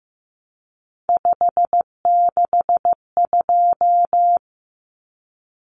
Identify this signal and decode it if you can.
Morse code, heard as a keyed tone, about 15 wpm, 700 Hz. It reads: 562
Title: Numbers - 3-Digit 15wpm